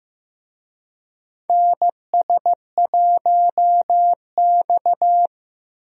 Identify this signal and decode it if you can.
Morse code, heard as a keyed tone, about 15 wpm, 700 Hz. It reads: NS1X
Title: USA Call Signs - 4 Characters - Repeated 2x - 15wpm